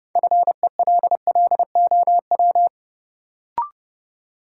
Morse code, heard as a keyed tone, about 31 wpm, 700 Hz. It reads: FELLOW
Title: Sets of 5 Words Encore - New Word Drill - 30wpm